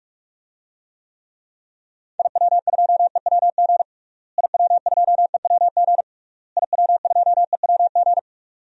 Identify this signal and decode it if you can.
Morse code, heard as a keyed tone, about 46 wpm, 700 Hz. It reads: IW2EWG IW2EWG IW2EWG
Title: DXCC-100 Call Signs - 6 Characters - Repeated 3x - 45wpm